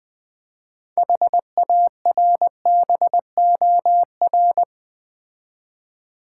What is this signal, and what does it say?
Morse code, heard as a keyed tone, about 20 wpm, 700 Hz. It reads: HARBOR
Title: Build A Word - 3 Letters At A Time - 6-Letter Words - 20wpm